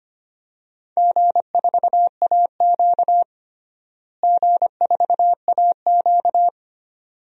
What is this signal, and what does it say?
Morse code, heard as a keyed tone, about 25 wpm, 700 Hz. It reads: G4AQ G4AQ
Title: DXCC-100 Call Signs - 4 Characters - Repeated 2x - 25wpm